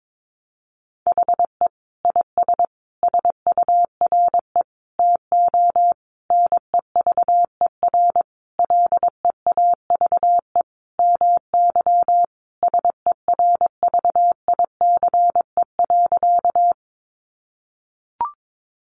Morse code, heard as a keyed tone, about 22 wpm, 700 Hz. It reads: HE IS SURE TO NEVER LEAVE MY SERVICE.
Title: Sentences from Top 700 Words 22wpm